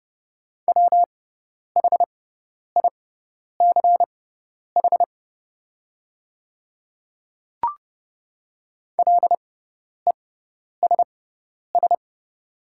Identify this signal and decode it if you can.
Morse code, heard as a keyed tone, about 30 wpm, spaced as for 10 wpm, 690 Hz. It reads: WHICH LESS
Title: 6x Character Spacing - Top 500 Words - 30wpm